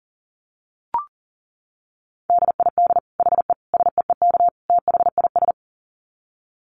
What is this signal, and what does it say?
Morse code, heard as a keyed tone, about 37 wpm, 700 Hz. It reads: DID HE SEEK THIS
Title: Dit Challenge 40wpm